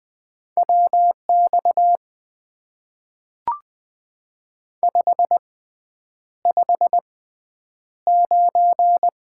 Morse code, heard as a keyed tone, about 20 wpm, spaced as for 9 wpm, 700 Hz. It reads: WX 559
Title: Top 90 QSO Elements - 6x Character Spacing - 20wpm